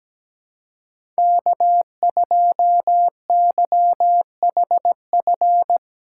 Morse code, heard as a keyed tone, about 17 wpm, 700 Hz. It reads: K2YHF
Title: USA Call Signs - 5 Characters - 3x Character Spacing - 17wpm